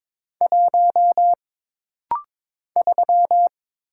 Morse code, heard as a keyed tone, about 22 wpm, 700 Hz. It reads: 1 3
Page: This rapid-fire timing is geared to push your ICR (Instant Character Recognition) skills.